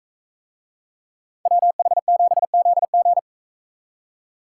Morse code, heard as a keyed tone, about 44 wpm, 700 Hz. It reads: WH7ZG